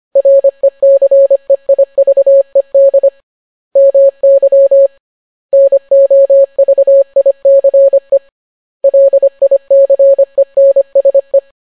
Morse code, heard as a keyed tone, about 25 wpm, 550 Hz. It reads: RECEIVED MY NOVICE LICENSE